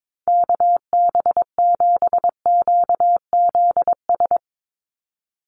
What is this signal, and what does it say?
Morse code, heard as a keyed tone, about 22 wpm, 700 Hz. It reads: KB7QZS
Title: USA Call Signs 22wpm